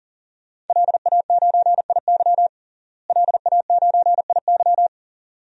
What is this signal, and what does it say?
Morse code, heard as a keyed tone, about 40 wpm, 700 Hz. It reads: LA9IY LA9IY
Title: DXCC-100 Call Signs - 5 Characters - Repeated 2x - 40wpm